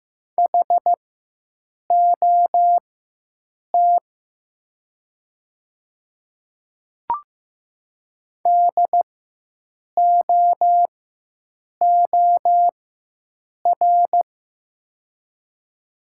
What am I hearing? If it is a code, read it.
Morse code, heard as a keyed tone, about 15 wpm, spaced as for 7 wpm, 700 Hz. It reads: HOT DOOR
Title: 4x Character Spacing - Top 300 Words - 15wpm